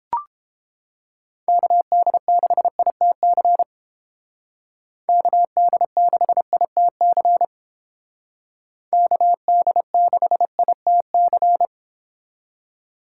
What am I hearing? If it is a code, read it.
Morse code, heard as a keyed tone, about 29 wpm, 700 Hz. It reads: KD6ITC KD6ITC KD6ITC